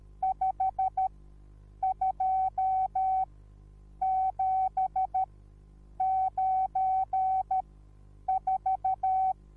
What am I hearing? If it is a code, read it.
Morse code, heard as a keyed tone, about 13 wpm, spaced as for 8 wpm, 760 Hz. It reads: 52794